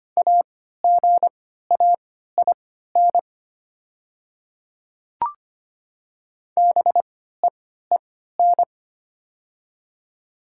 Morse code, heard as a keyed tone, about 25 wpm, spaced as for 14 wpm, 700 Hz. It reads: AGAIN BEEN
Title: 3x Character Spacing - Top 200 Words - 25wpm